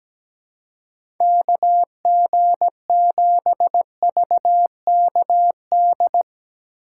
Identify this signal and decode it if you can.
Morse code, heard as a keyed tone, about 17 wpm, 700 Hz. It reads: KG7VKD
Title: USA Call Signs - 6 Characters - 4x Character Spacing - 17wpm